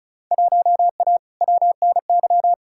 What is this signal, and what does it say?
Morse code, heard as a keyed tone, about 34 wpm, 700 Hz. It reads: 1A WNY
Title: Field Day Exchanges Encore - 35wpm